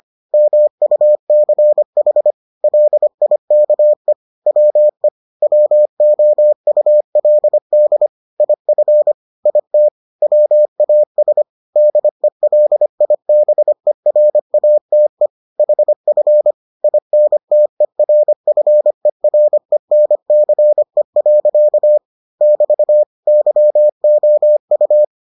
Morse code, heard as a keyed tone, about 25 wpm, 600 Hz. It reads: MUCH LIKE WE WOULD IF IT WAS DELIBERATE HF INTERFERENCE. <BT> YOU